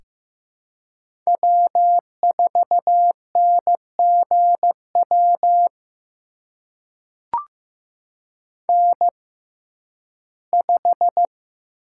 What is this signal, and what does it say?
Morse code, heard as a keyed tone, about 15 wpm, spaced as for 11 wpm, 700 Hz. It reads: W4NGW N5
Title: USA Call Signs - 5 Characters - 6x Character Spacing - 15wpm